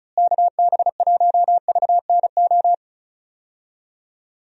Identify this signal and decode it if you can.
Morse code, heard as a keyed tone, about 34 wpm, 700 Hz. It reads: KB1VNO
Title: USA Call Signs - 6 Characters - Repeated 3x - 35wpm